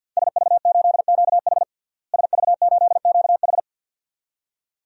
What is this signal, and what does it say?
Morse code, heard as a keyed tone, about 47 wpm, 700 Hz. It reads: SV8QH SV8QH
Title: DXCC-100 Call Signs - 5 Characters - Repeated 2x - 50wpm